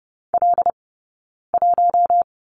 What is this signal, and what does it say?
Morse code, heard as a keyed tone, about 30 wpm, 700 Hz. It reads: L 1
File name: Lesson 20d - Number 3 - Two Characters 30wpm